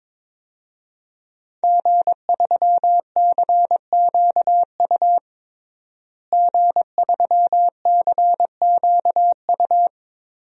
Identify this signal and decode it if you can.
Morse code, heard as a keyed tone, about 22 wpm, 700 Hz. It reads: G3CQU G3CQU
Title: DXCC-100 Call Signs - 5 Characters - Repeated 2x - 22wpm